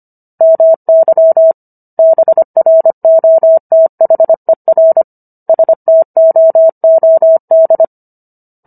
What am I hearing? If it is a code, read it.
Morse code, heard as a keyed tone, about 25 wpm, 650 Hz. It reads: MY BROTHER STOOD